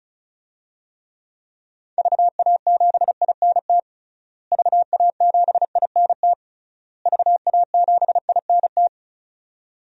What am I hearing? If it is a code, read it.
Morse code, heard as a keyed tone, about 36 wpm, 700 Hz. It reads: VA7INT VA7INT VA7INT
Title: DXCC-100 Call Signs - 6 Characters - Repeated 3x - 35wpm